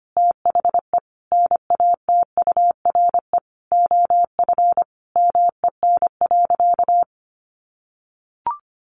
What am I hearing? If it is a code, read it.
Morse code, heard as a keyed tone, about 25 wpm, 700 Hz. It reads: THE NATURE OF MEN.
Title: Sentences from Top 1500 Words 25wpm